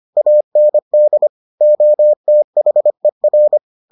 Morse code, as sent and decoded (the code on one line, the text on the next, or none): .- -. -.. / --- - .... .-.
AND OTHER